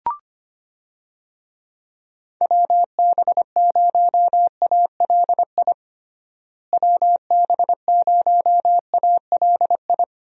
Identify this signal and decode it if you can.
Morse code, heard as a keyed tone, about 25 wpm, 700 Hz. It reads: WB0ALI WB0ALI
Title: USA Call Signs - 6 Characters - Repeated 2x - 25wpm